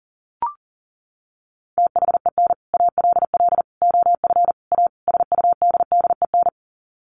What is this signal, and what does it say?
Morse code, heard as a keyed tone, about 40 wpm, 700 Hz. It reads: THEN ALL OF A SUDDEN
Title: Sets of 5 Words Encore 40wpm